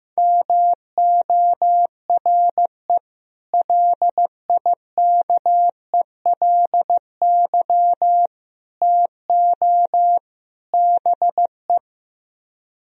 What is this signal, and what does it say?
Morse code, heard as a keyed tone, about 15 wpm, 700 Hz. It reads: MORE LIKELY TO BE
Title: Speed-Racing - Sets of 4 Words 15wpm